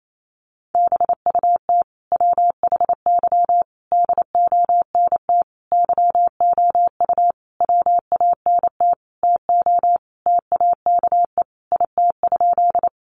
Morse code, heard as a keyed tone, about 28 wpm, 700 Hz. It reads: BUT WHY DONT YOU WANT TO TAKE IT?
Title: Sentences from Top 300 Words 28wpm